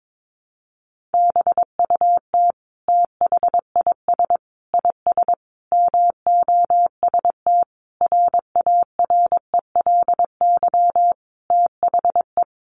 Morse code, heard as a keyed tone, about 22 wpm, 700 Hz. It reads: BUT THIS IS MOST RARELY THE
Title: Sentences from Top 1400 Words 22wpm